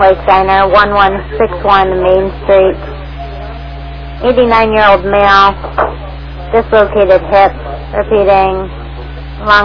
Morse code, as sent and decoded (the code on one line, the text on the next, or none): none